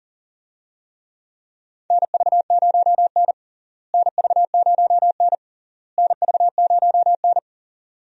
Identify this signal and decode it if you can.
Morse code, heard as a keyed tone, about 40 wpm, 700 Hz. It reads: NV0N NV0N NV0N
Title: USA Call Signs - 4 Characters - Repeated 3x - 40wpm